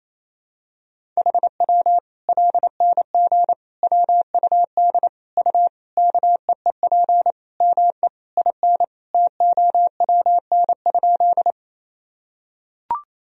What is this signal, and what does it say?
Morse code, heard as a keyed tone, about 28 wpm, 700 Hz. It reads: HW LNG WUD U KEEP ME IN TOWN?
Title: Sentences from Top 500 Words and Abbreviations 28wpm